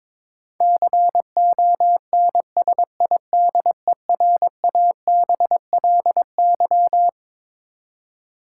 Morse code, heard as a keyed tone, about 22 wpm, 700 Hz. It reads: CONSIDERABLY
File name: Twelve Letter Words - 22wpm